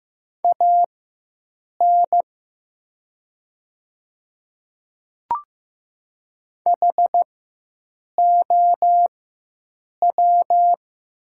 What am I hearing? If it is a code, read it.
Morse code, heard as a keyed tone, about 15 wpm, spaced as for 7 wpm, 700 Hz. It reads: AN HOW
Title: 4x Character Spacing - Top 100 Words - 15wpm